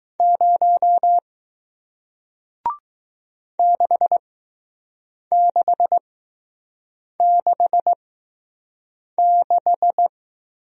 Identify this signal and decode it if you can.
Morse code, heard as a keyed tone, about 20 wpm, spaced as for 6 wpm, 700 Hz. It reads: 0 6666